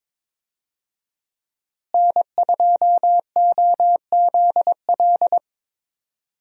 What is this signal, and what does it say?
Morse code, heard as a keyed tone, about 22 wpm, 700 Hz. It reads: N2OZL
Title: USA Call Signs - 5 Characters - Repeated 3x - 22wpm